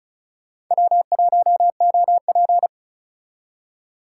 Morse code, heard as a keyed tone, about 44 wpm, 700 Hz. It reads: W1OP